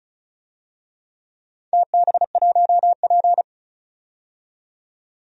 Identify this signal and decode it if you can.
Morse code, heard as a keyed tone, about 36 wpm, 700 Hz. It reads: TB1P
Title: DXCC-100 Call Signs - 4 Characters - 35wpm